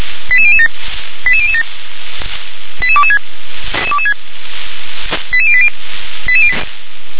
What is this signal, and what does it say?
Five-tone selective call (ZVEI), heard as quick runs of five tones, tones of about 70 ms. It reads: 88007 88007 88226 88080
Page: ZVEI